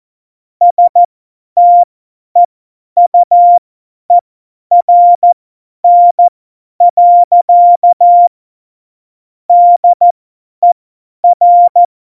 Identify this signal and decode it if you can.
Morse code, heard as a keyed tone, about 14 wpm, spaced as for 10 wpm, 700 Hz. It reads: STEUERN. DER
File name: Morsekurs / Witze / Witz 22. Dezember 2020, 50 F70